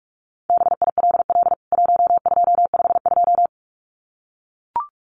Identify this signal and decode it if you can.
Morse code, heard as a keyed tone, about 44 wpm, 700 Hz. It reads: BILL 1252
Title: CWT Exchange 45wpm